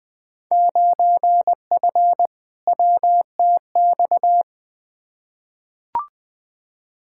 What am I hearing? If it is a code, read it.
Morse code, heard as a keyed tone, about 20 wpm, 700 Hz. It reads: 9F WTX
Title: Field Day Exchanges Encore - 20wpm